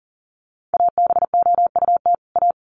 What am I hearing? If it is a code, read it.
Morse code, heard as a keyed tone, about 40 wpm, 700 Hz. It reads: ABOUT A
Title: Sets of 2 Words 40wpm